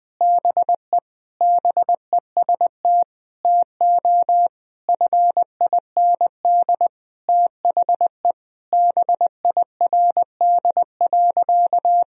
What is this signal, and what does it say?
Morse code, heard as a keyed tone, about 21 wpm, 700 Hz. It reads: BE BEST TO FIND THE BIRD.
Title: Sentences from Top 1000 Words - Review Part 2 - 20wpm